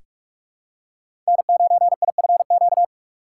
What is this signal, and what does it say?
Morse code, heard as a keyed tone, about 42 wpm, 700 Hz. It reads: N9IFQ
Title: USA Call Signs - 5 Characters - 3x Character Spacing - 45wpm